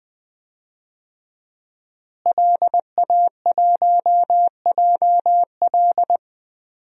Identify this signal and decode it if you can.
Morse code, heard as a keyed tone, about 21 wpm, 700 Hz. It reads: LA1JL
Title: DXCC-100 Call Signs - 5 Characters - Repeated 2x - 20wpm